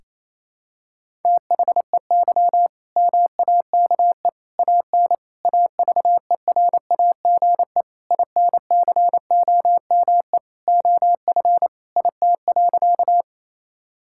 Morse code, heard as a keyed tone, about 28 wpm, 700 Hz. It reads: THEY MAKE AN AVERAGE INCOME OF IT.
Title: Sentences from Top 1000 Words - Review Part 6 - 28wpm